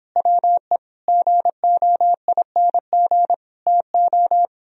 Morse code, heard as a keyed tone, about 31 wpm, 700 Hz. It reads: WE GOING TO